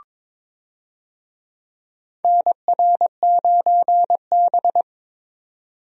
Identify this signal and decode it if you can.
Morse code, heard as a keyed tone, about 22 wpm, 700 Hz. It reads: NR9B
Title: USA Call Signs - 4 Characters - 22wpm